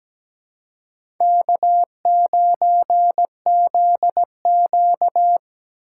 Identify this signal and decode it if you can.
Morse code, heard as a keyed tone, about 17 wpm, 700 Hz. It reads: K9ZQ
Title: USA Call Signs - 4 Characters - 6x Character Spacing - 17wpm